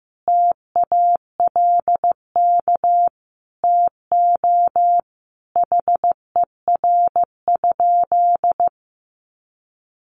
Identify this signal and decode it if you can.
Morse code, heard as a keyed tone, about 15 wpm, 700 Hz. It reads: TALK TO HER?
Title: Sentences from Top 1400 Words 15wpm